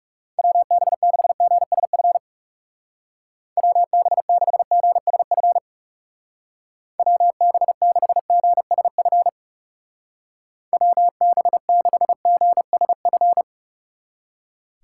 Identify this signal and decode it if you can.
Morse code, heard as a keyed tone, about 36 wpm, 700 Hz. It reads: WB6GSF WB6GSF WB6GSF WB6GSF
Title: Speed Racing - USA Call Signs 30wpm